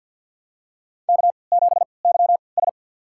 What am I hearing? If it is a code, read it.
Morse code, heard as a keyed tone, about 50 wpm, spaced as for 28 wpm, 700 Hz. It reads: K7YS
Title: USA Call Signs - 4 Characters - 3x Character Spacing - 50wpm